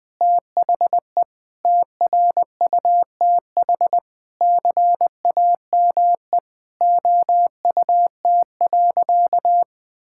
Morse code, heard as a keyed tone, about 20 wpm, 700 Hz. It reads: THE TRUTH CAME OUT.
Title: Sentences with 9-Word Phrases - 20wpm